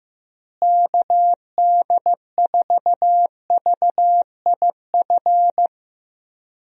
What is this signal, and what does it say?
Morse code, heard as a keyed tone, about 15 wpm, 700 Hz. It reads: KD4VIF
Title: USA Call Signs - 6 Characters - Repeated 3x - 15wpm